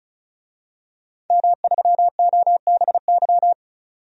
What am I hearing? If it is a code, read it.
Morse code, heard as a keyed tone, about 36 wpm, 700 Hz. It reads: M3OBY
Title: DXCC-100 Call Signs - 5 Characters - 35wpm